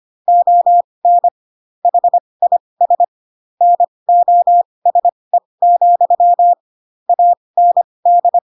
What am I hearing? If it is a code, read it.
Morse code, heard as a keyed tone, about 25 wpm, spaced as for 20 wpm, 700 Hz. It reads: ON HIS NOSE, AND
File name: Pinocchio / Pinocchio ch2 15 25 ik1hge